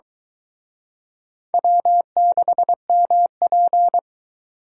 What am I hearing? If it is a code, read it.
Morse code, heard as a keyed tone, about 23 wpm, 700 Hz. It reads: W6MP